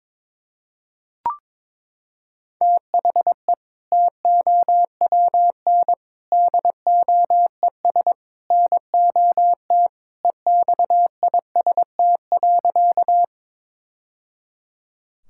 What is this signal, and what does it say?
Morse code, heard as a keyed tone, about 22 wpm, 700 Hz. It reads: THE TOWN DOES NOT EXIST.
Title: Sentences from Top 1000 Words - Review Part 5 - 22wpm